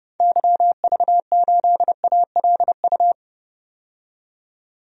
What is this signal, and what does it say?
Morse code, heard as a keyed tone, about 30 wpm, 700 Hz. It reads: YV8ALU
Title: DXCC-100 Call Signs - 6 Characters - 30wpm